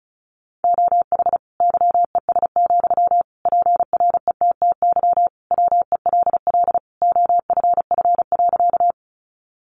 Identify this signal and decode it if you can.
Morse code, heard as a keyed tone, about 34 wpm, 700 Hz. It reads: OH YES, PRETTY WELL OFF.
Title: Sentences from Top 1000 Words 35wpm